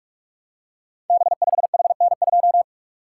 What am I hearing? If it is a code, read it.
Morse code, heard as a keyed tone, about 44 wpm, 700 Hz. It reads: B5HN2
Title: Code Groups - 5 Letter:Numbers - 45wpm